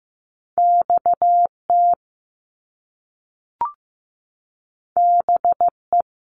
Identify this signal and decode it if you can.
Morse code, heard as a keyed tone, about 15 wpm, 700 Hz. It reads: XT BE
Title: Top Two Letter Combinations in Words 15wpm